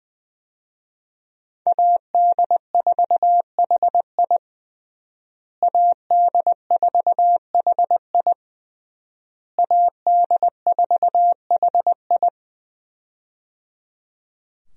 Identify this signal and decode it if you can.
Morse code, heard as a keyed tone, about 20 wpm, 700 Hz. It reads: AD4HI AD4HI AD4HI
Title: USA Call Signs - 5 Characters - Repeated 3x - 20wpm